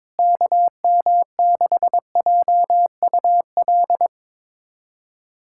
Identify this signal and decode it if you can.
Morse code, heard as a keyed tone, about 22 wpm, 700 Hz. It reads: KM6JUL
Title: USA Callsign DE Callsign 22wpm